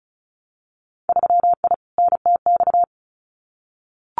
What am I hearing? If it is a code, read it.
Morse code, heard as a keyed tone, about 34 wpm, 700 Hz. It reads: 3I NTX
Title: Winter Field Day Exchanges 35wpm